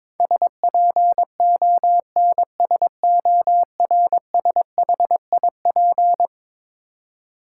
Morse code, heard as a keyed tone, about 22 wpm, 700 Hz. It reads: SPONSORSHIP
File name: practice / 22wpm / Eleven Letter Words - 22wpm